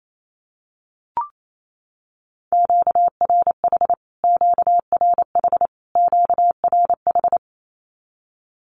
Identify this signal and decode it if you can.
Morse code, heard as a keyed tone, about 28 wpm, 700 Hz. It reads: QRH QRH QRH
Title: Q-Codes Repeated 3x 28wpm